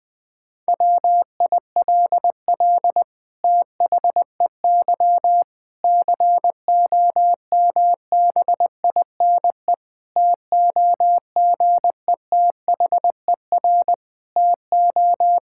Morse code, heard as a keyed tone, about 20 wpm, 700 Hz. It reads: WILL THEY COMBINE TOGETHER TO
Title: Sentences from Top 1600 Words 20wpm